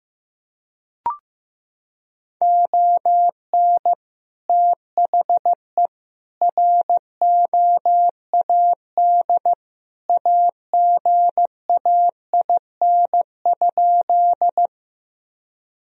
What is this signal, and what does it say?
Morse code, heard as a keyed tone, about 15 wpm, 700 Hz. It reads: ON THE ROAD AGAIN?
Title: Sentences from Top 1000 Words - Review Part 2 - 15wpm